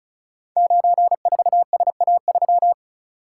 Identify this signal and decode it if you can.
Morse code, heard as a keyed tone, about 36 wpm, 700 Hz. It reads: 94SA3
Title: Code Groups - 5 Letter:Numbers - 35wpm